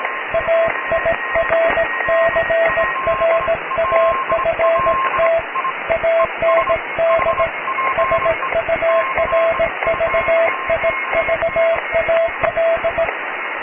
Morse code, heard as a keyed tone, about 18 wpm, 660 Hz, with radio noise. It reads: AIRCRAFT AND SURVIVAL
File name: cw / tekst(b)+QRM